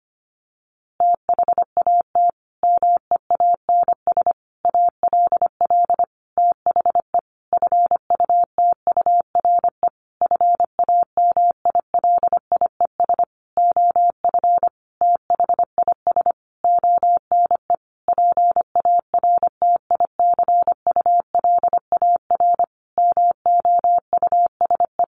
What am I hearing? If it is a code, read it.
Morse code, heard as a keyed tone, about 25 wpm, 700 Hz. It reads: THAT MEANS ALL THE FUTURE FAMILIES OF THIS ONE PARTICULAR MOUSE
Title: A Sound Of Thunder 25wpm